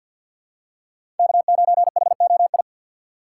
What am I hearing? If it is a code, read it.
Morse code, heard as a keyed tone, about 51 wpm, 700 Hz. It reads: K9HOI
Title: USA Call Signs - 5 Characters - 4x Character Spacing - 50wpm